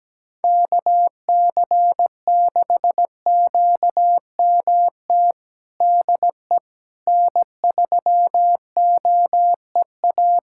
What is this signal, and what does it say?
Morse code, heard as a keyed tone, about 17 wpm, 700 Hz. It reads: KC6QMT DE N3OEA
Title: USA Callsign DE Callsign 17wpm